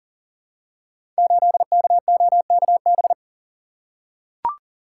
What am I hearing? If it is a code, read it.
Morse code, heard as a keyed tone, about 39 wpm, 690 Hz. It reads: 8KOKB
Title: Code Groups - 5 Letter:Numbers - 40wpm